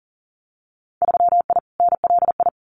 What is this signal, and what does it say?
Morse code, heard as a keyed tone, about 40 wpm, 700 Hz. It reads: 3I NLI
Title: Winter Field Day Exchanges Encore 40wpm